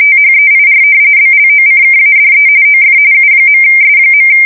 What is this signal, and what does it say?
RTTY, about 45.45 baud. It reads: THEIR BLOCKADE OF U.S.
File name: fsk170 / usnavy2